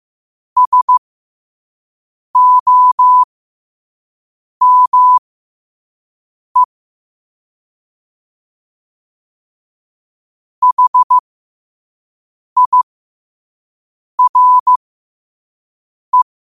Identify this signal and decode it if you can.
Morse code, heard as a keyed tone, about 15 wpm, spaced as for 5 wpm, 1000 Hz. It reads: SOME HIRE
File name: qotdfiles / 5wpm